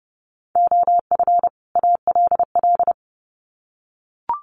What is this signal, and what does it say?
Morse code, heard as a keyed tone, about 30 wpm, 700 Hz. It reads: OF ALL
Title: Sets of 2 Words 30wpm